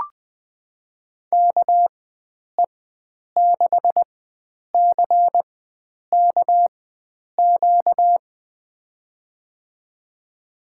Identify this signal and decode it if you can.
Morse code, heard as a keyed tone, about 20 wpm, spaced as for 9 wpm, 700 Hz. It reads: KE6CKQ
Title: USA Call Signs - 6 Characters - 4x Character Spacing - 20wpm